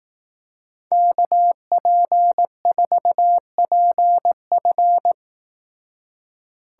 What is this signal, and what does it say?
Morse code, heard as a keyed tone, about 19 wpm, 700 Hz. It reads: KP4PF
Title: Speed Racing - USA 2x2 Call Signs 15wpm